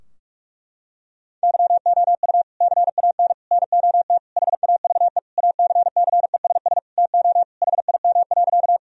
Morse code, heard as a keyed tone, about 46 wpm, 700 Hz. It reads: YOU CAN NOT HAVE ACCESS TO HIM.
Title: Sentences from Top 1000 Words - Review Part 4 - 45wpm